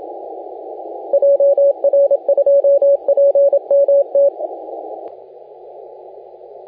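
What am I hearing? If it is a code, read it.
Morse code, heard as a keyed tone, about 27 wpm, 560 Hz, with radio noise. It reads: JR2PMT
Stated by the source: Morse contest audio recorded by LA4C during the SAC CW 2018 contest.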